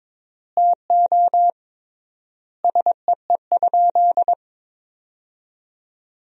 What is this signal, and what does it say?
Morse code, heard as a keyed tone, about 22 wpm, 700 Hz. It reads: TO SEE?
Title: 3x Word-Spacing - Sentences from Top 100 Words - 22wpm